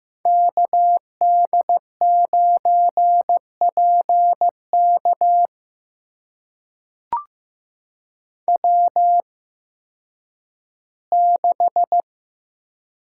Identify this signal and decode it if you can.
Morse code, heard as a keyed tone, about 15 wpm, spaced as for 10 wpm, 700 Hz. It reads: KD9PK W6
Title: USA Call Signs - 5 Characters - 8x Character Spacing - 15wpm